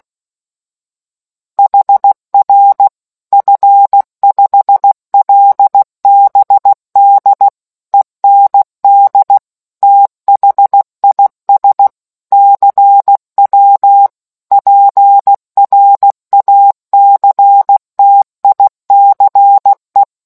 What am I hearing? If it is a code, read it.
Morse code, heard as a keyed tone, about 16 wpm, 800 Hz. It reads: HR F5LBD END THIS CW PRACTICE